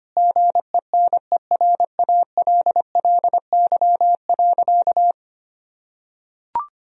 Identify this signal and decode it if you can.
Morse code, heard as a keyed tone, about 25 wpm, 690 Hz. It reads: GENERALLY.
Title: Sentences from Top 1000 Words - Review Part 4 - 25wpm